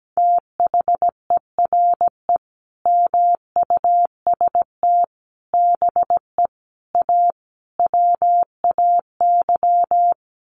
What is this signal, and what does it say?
Morse code, heard as a keyed tone, about 17 wpm, 700 Hz. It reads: THERE MUST BE A WAY
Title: Sentences from Top 1400 Words 17wpm